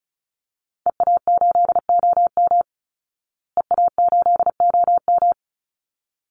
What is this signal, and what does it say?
Morse code, heard as a keyed tone, about 34 wpm, 700 Hz. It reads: EA8OM EA8OM
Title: Call Signs Repeated 3x - 35wpm